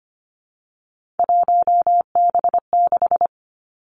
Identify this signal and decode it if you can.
Morse code, heard as a keyed tone, about 25 wpm, 700 Hz. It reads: 1B6
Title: Three Challenging Letter-Number Combinations 25wpm